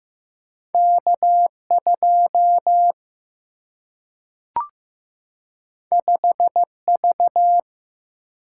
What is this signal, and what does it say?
Morse code, heard as a keyed tone, about 15 wpm, 700 Hz. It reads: K 2 5 V
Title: Single-Letter Single-Number 15wpm